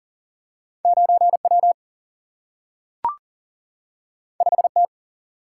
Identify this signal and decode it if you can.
Morse code, heard as a keyed tone, about 39 wpm, 700 Hz. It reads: 9 W 5 T
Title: Single-Letter Single-Number 40wpm